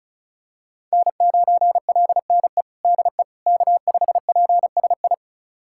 Morse code, heard as a keyed tone, about 36 wpm, 700 Hz. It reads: N9LNE DE K5PSI
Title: USA Callsign DE Callsign - 5 Characters - 35wpm